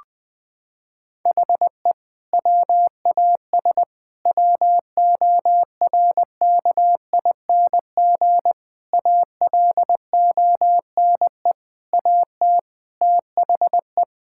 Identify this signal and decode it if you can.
Morse code, heard as a keyed tone, about 20 wpm, 700 Hz. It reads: HE WAS WORKING ALONE AT THE
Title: Sentences from Top 1000 Words - Review Part 5 - 20wpm